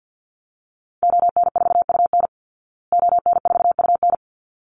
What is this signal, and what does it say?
Morse code, heard as a keyed tone, about 52 wpm, 700 Hz. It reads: ON4UN ON4UN
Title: Call Signs Repeated 3x - 50wpm